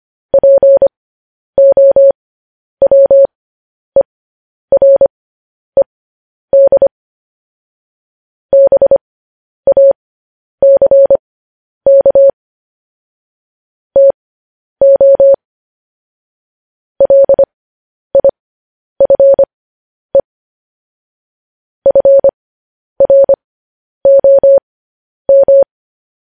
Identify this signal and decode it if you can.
Morse code, heard as a keyed tone, about 25 wpm, spaced as for 10 wpm, 550 Hz. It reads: POWERED BACK TO LIFE FROM
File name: podcast / News headlines-1x-25x10